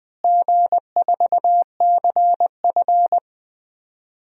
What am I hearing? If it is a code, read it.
Morse code, heard as a keyed tone, about 20 wpm, 700 Hz. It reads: G4CF
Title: DXCC-100 Call Signs - 4 Characters - Repeated 3x - 20wpm